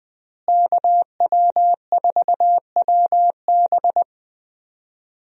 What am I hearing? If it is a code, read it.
Morse code, heard as a keyed tone, about 20 wpm, 700 Hz. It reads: KW4WB
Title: USA Call Signs - 5 Characters - Repeated 2x - 20wpm